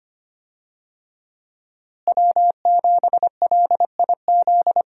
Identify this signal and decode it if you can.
Morse code, heard as a keyed tone, about 25 wpm, 700 Hz. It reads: W7LIZ
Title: USA Call Signs - 5 Characters - Repeated 2x - 25wpm